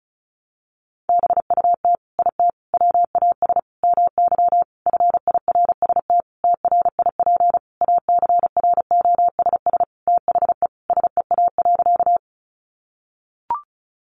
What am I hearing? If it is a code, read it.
Morse code, heard as a keyed tone, about 34 wpm, 700 Hz. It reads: BUT IT WAS MY FIRST TRIP ACROSS THE SEA.
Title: Sentences from Top 1300 Words 35wpm